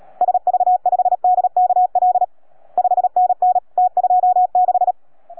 Morse code, heard as a keyed tone, about 35 wpm, 720 Hz. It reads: SV5DKL 5NN T26